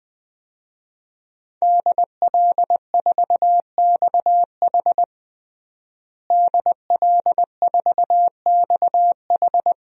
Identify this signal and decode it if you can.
Morse code, heard as a keyed tone, about 21 wpm, 700 Hz. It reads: DL4XH DL4XH
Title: DXCC-100 Call Signs - 5 Characters - Repeated 3x - 20wpm